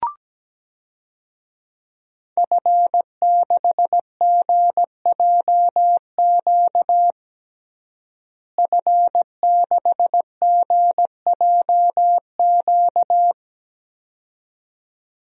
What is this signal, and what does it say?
Morse code, heard as a keyed tone, about 17 wpm, 700 Hz. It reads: F6GJQ F6GJQ
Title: DXCC-100 Call Signs - 5 Characters - Repeated 2x - 17wpm